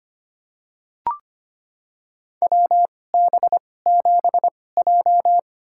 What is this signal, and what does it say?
Morse code, heard as a keyed tone, about 25 wpm, spaced as for 18 wpm, 700 Hz. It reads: WB7J
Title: USA Call Signs - 4 Characters - 2x Character Spacing - 25wpm